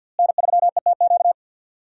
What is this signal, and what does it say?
Morse code, heard as a keyed tone, about 52 wpm, 700 Hz. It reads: N3ETQ
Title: USA Call Signs - 5 Characters - Repeated 2x - 50wpm